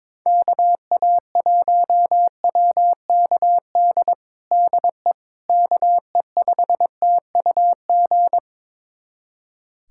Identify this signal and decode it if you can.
Morse code, heard as a keyed tone, about 22 wpm, 700 Hz. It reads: KA1WKD DE KE5TUG
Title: USA Callsign DE Callsign - 6 Characters - 22wpm